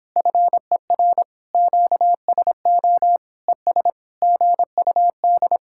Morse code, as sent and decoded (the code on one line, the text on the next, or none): ..-. .-. / --.- ... --- / . ... / --. ..- -..
FER QSO ES GUD